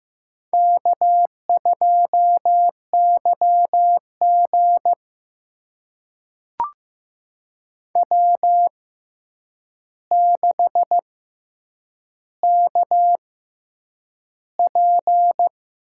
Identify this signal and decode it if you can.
Morse code, heard as a keyed tone, about 15 wpm, spaced as for 8 wpm, 700 Hz. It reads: K2YG W6KP
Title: USA Call Signs - 4 Characters - 6x Character Spacing - 15wpm